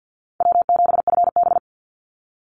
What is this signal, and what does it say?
Morse code, heard as a keyed tone, about 48 wpm, 700 Hz. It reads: W7FB
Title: Easy Call Signs Repeated 2x - 50wpm